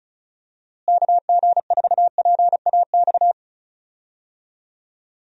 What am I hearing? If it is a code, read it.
Morse code, heard as a keyed tone, about 36 wpm, 700 Hz. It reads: KG4PAX
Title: USA Call Signs - 6 Characters - Repeated 2x - 35wpm